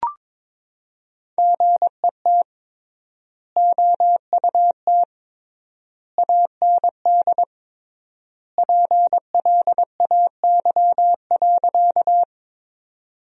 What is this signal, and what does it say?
Morse code, heard as a keyed tone, about 23 wpm, 700 Hz. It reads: GET OUT AND PLAY.
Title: 3x Word-Spacing - Sentences from Top 200 Words - 22wpm